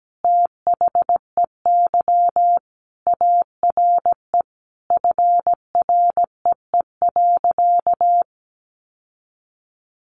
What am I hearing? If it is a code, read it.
Morse code, heard as a keyed tone, about 17 wpm, 700 Hz. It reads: THEY ARE FREE.
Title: Sentences from Top 1400 Words 17wpm